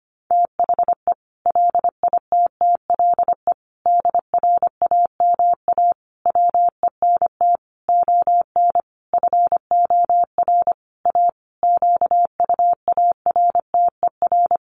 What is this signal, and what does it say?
Morse code, heard as a keyed tone, about 25 wpm, 700 Hz. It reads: THE LITTLE DRAMA WENT ON FOR A QUARTER
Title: Sentences from Top 1500 Words 25wpm